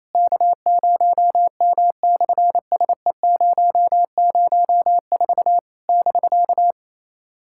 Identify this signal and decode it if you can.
Morse code, heard as a keyed tone, about 28 wpm, 700 Hz. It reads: K0M/SE004 <BK>
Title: SOTA QSO Practice 28wpm